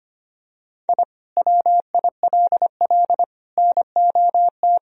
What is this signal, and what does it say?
Morse code, heard as a keyed tone, about 25 wpm, 700 Hz. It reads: I WILL NOT
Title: Sentences from Top 1000 Words - Review Part 1 - 25wpm